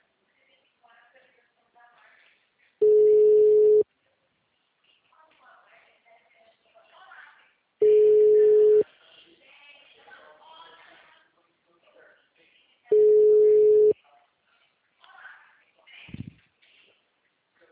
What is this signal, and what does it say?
Ringback tone (Europe, 425 Hz), with speech.